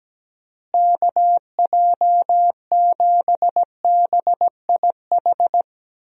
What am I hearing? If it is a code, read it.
Morse code, heard as a keyed tone, about 17 wpm, 700 Hz. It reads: KJ7BIH
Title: USA Call Signs - 6 Characters - Repeated 3x - 17wpm